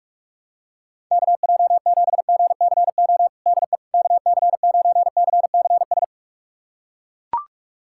Morse code, heard as a keyed tone, about 41 wpm, 700 Hz. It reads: KJ7GCO DE KC9CCS
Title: USA Callsign DE Callsign 45wpm